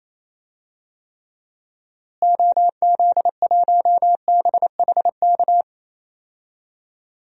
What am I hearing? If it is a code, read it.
Morse code, heard as a keyed tone, about 28 wpm, 700 Hz. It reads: OZ1BHK
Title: DXCC-100 Call Signs - 6 Characters - Repeated 3x - 28wpm